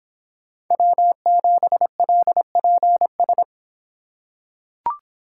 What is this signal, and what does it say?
Morse code, heard as a keyed tone, about 24 wpm, 700 Hz. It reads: W7LPS